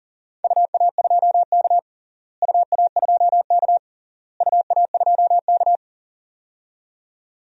Morse code, heard as a keyed tone, about 40 wpm, 700 Hz. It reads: UA2K UA2K UA2K
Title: DXCC-100 Call Signs - 4 Characters - Repeated 3x - 40wpm